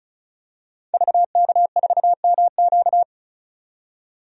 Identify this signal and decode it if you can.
Morse code, heard as a keyed tone, about 36 wpm, 700 Hz. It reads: VK4MQ
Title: DXCC-100 Call Signs - 5 Characters - 35wpm